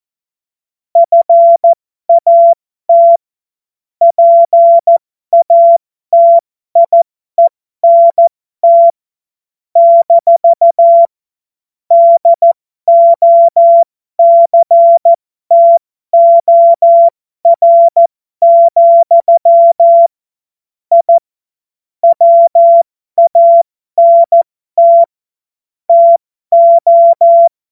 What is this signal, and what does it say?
Morse code, heard as a keyed tone, about 14 wpm, spaced as for 12 wpm, 680 Hz. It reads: FAT PATIENT - DOCTOR, I WANT TO
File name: Morsekurs / Witze